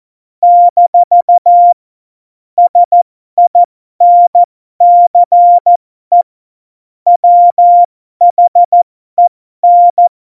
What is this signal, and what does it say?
Morse code, heard as a keyed tone, about 14 wpm, spaced as for 12 wpm, 700 Hz. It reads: - SINCE WHEN
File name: Morsekurs / Witze